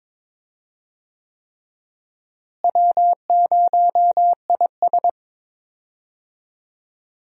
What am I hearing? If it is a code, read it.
Morse code, heard as a keyed tone, about 22 wpm, 700 Hz. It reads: W0IS
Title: USA Call Signs - 4 Characters - Repeated 3x - 22wpm